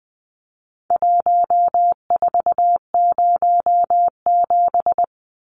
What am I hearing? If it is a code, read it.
Morse code, heard as a keyed tone, about 20 wpm, 700 Hz. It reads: 1407
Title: Numbers - Contest Style - 4-Digit 20wpm